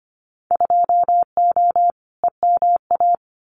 Morse code, heard as a keyed tone, about 25 wpm, 700 Hz. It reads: 2O EMA
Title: Winter Field Day Exchanges 25wpm